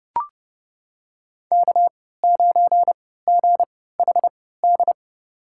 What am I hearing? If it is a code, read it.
Morse code, heard as a keyed tone, about 29 wpm, spaced as for 17 wpm, 700 Hz. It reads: K9GHD
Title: USA Call Signs - 5 Characters - 3x Character Spacing - 30wpm